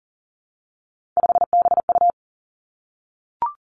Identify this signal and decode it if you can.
Morse code, heard as a keyed tone, about 38 wpm, 700 Hz. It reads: 5BU
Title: Three Challenging Letter-Number Combinations 40wpm